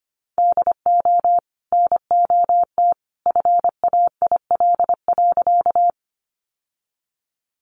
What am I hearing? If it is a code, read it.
Morse code, heard as a keyed tone, about 25 wpm, 700 Hz. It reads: DO NOT FAIL.
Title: Sentences from Top 1100 Words 25wpm